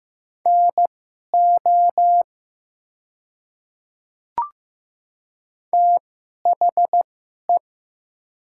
Morse code, heard as a keyed tone, about 15 wpm, spaced as for 11 wpm, 700 Hz. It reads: NO THE
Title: 2x Character Spacing - Top 100 Words - 15wpm